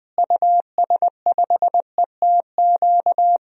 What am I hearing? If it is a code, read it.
Morse code, heard as a keyed tone, about 20 wpm, 700 Hz. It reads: US5ETQ
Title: DXCC-100 Call Signs - 6 Characters - Repeated 3x - 20wpm